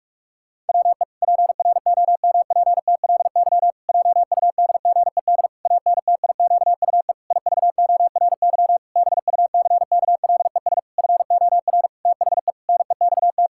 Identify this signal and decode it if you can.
Morse code, heard as a keyed tone, about 46 wpm, 700 Hz. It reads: WE PROMPTLY JUDGED ANTIQUE IVORY BUCKLES FOR THE NEXT
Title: Pangram Challenge v2 - 45wpm